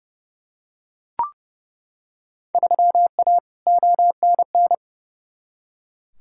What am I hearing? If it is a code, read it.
Morse code, heard as a keyed tone, about 30 wpm, 700 Hz. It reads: 3A ONN
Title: Field Day Exchanges Encore - 30wpm